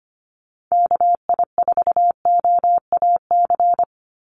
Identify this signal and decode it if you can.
Morse code, heard as a keyed tone, about 25 wpm, 700 Hz. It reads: KI4OAC
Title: USA Call Signs 25wpm